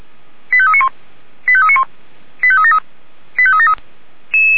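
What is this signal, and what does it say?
Five-tone selective call (ZVEI), heard as quick runs of five tones, tones of about 70 ms. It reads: 85391 85391 85373 85373
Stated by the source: ZVEI five-tone selective call received by SDR.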